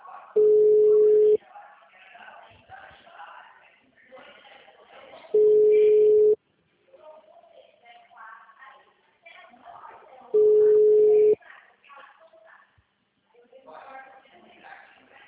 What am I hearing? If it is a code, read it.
Ringback tone (Europe, 425 Hz), with speech.